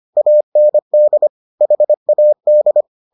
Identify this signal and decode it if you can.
Morse code, heard as a keyed tone, about 25 wpm, 600 Hz. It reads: AND HAD